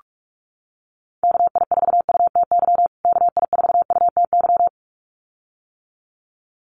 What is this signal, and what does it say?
Morse code, heard as a keyed tone, about 44 wpm, 700 Hz. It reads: KI4UTY KI4UTY
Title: USA Call Signs Repeated 2x 45wpm